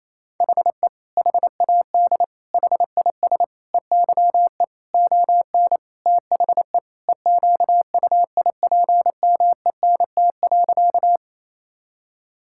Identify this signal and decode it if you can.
Morse code, heard as a keyed tone, about 28 wpm, 700 Hz. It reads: HE HAD HIS EYE ON THE EQUIPMENT.
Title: Sentences from Top 1000 Words - Review Part 5 - 28wpm